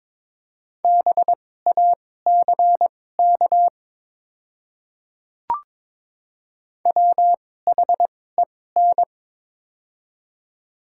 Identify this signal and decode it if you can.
Morse code, heard as a keyed tone, about 22 wpm, spaced as for 16 wpm, 700 Hz. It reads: BACK WHEN
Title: 2x Character Spacing - Top 300 Words - 22wpm